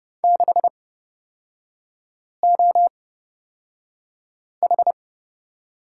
Morse code, heard as a keyed tone, about 30 wpm, spaced as for 5 wpm, 700 Hz. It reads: 6OH
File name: Lesson 37e - Letter x - Three Characters 30wpm